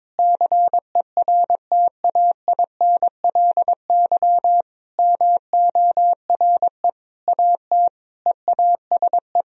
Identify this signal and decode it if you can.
Morse code, heard as a keyed tone, about 22 wpm, 700 Hz. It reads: CERTAINLY MORE AT EASE
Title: Sentences from Top 1000 Words - Review Part 1 - 22wpm